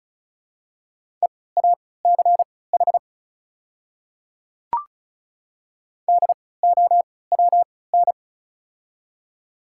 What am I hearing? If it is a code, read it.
Morse code, heard as a keyed tone, about 34 wpm, spaced as for 20 wpm, 700 Hz. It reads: EACH DOWN
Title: 3x Character Spacing - Top 100 Words - 35wpm